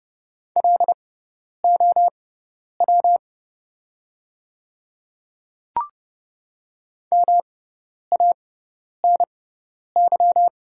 Morse code, heard as a keyed tone, about 30 wpm, spaced as for 10 wpm, 700 Hz. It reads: LOW MANY
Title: 6x Character Spacing - Top 300 Words - 30wpm